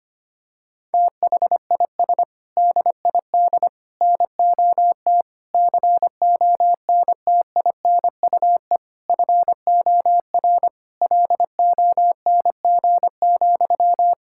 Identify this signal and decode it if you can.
Morse code, heard as a keyed tone, about 25 wpm, 700 Hz. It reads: THIS DID NOT CONTINUE FOR LONG,
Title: Sentences from Top 1000 Words - Review Part 6 - 25wpm